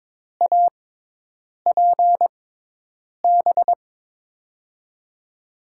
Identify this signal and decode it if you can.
Morse code, heard as a keyed tone, about 22 wpm, spaced as for 8 wpm, 700 Hz. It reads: APB
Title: USA Call Signs - 6 Characters - 6x Character Spacing - 22wpm